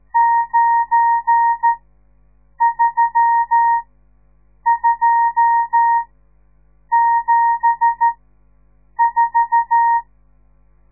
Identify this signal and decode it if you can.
Morse code, heard as a keyed tone, about 14 wpm, spaced as for 7 wpm, 930 Hz. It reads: 93274
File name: radiomorningstar / morse / les 10